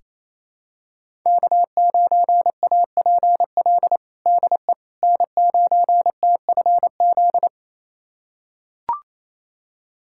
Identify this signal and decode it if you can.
Morse code, heard as a keyed tone, about 27 wpm, 700 Hz. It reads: K9APL DE N9TFZ
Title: USA Callsign DE Callsign - 5 Characters - 28wpm